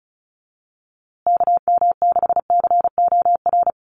Morse code, heard as a keyed tone, about 34 wpm, 700 Hz. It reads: KM6COR
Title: USA Call Signs Repeated 3x 35wpm